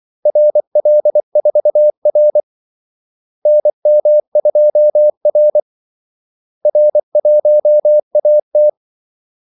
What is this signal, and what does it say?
Morse code, heard as a keyed tone, about 24 wpm, 600 Hz. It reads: RL4R NM2R R1AT
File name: Callsigns-24wpm